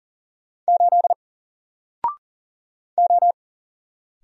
Morse code, heard as a keyed tone, about 40 wpm, 700 Hz. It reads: 8 O
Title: Single Letter-Number Alt 40wpm